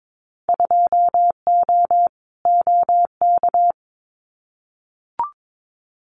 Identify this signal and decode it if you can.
Morse code, heard as a keyed tone, about 22 wpm, 700 Hz. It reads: 2O OK
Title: Winter Field Day Exchanges Encore 22wpm